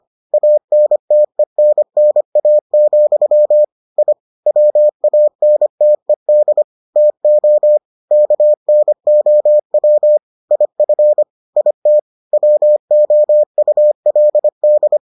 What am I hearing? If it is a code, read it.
Morse code, heard as a keyed tone, about 25 wpm, 600 Hz. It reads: ANTENNA, I WANTED TO KNOW IF IT WOULD